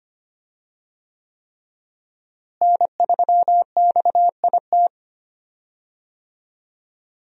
Morse code, heard as a keyed tone, about 25 wpm, 700 Hz. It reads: N3XIT
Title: USA Call Signs - 5 Characters - Repeated 3x - 25wpm